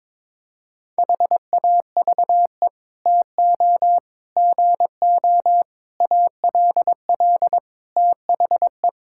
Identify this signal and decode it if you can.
Morse code, heard as a keyed tone, about 21 wpm, 700 Hz. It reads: HAVE TO GO ALL THE
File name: practice / 22wpm / Sets of 8 Words - 22wpm